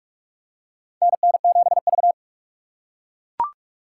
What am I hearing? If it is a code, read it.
Morse code, heard as a keyed tone, about 42 wpm, 700 Hz. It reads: NN7V
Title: USA Call Signs - 6x Character Spacing - 45wpm